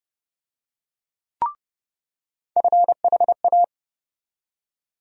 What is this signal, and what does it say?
Morse code, heard as a keyed tone, about 30 wpm, 700 Hz. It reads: FHA
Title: USA Call Sign Postfixes - 3 Letters 30wpm